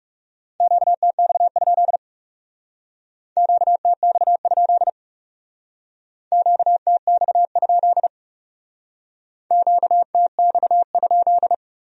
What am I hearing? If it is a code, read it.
Morse code, heard as a keyed tone, about 38 wpm, 700 Hz. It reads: QTX? QTX? QTX? QTX?